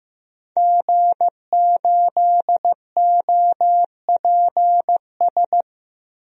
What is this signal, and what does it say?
Morse code, heard as a keyed tone, about 15 wpm, 700 Hz. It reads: G8OPS
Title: DXCC-100 Call Signs - 5 Characters - Repeated 2x - 15wpm